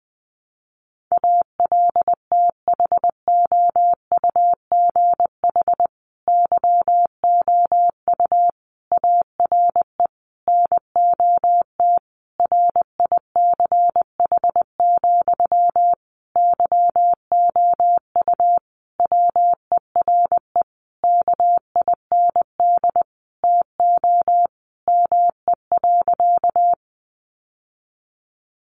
Morse code, heard as a keyed tone, about 20 wpm, 700 Hz. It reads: ALTHOUGH YOU ARE NOT RICH, YOU WERE KIND TO ME.
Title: Sentences from Top 1100 Words 20wpm